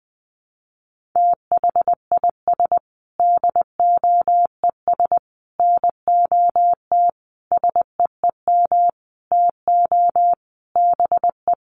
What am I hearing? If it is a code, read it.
Morse code, heard as a keyed tone, about 20 wpm, 700 Hz. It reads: THIS DOES NOT SEEM TO BE
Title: Sentences from Top 1400 Words 20wpm